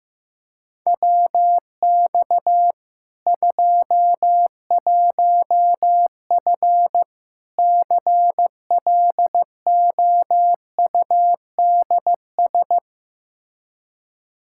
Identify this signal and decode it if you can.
Morse code, heard as a keyed tone, about 15 wpm, 700 Hz. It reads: WX 21F CLOUDS
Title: K1USN SST QSO Practice 15wpm